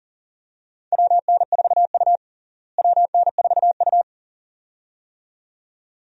Morse code, heard as a keyed tone, about 40 wpm, 700 Hz. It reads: WN4U WN4U
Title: USA Call Signs - 4 Characters - Repeated 2x - 40wpm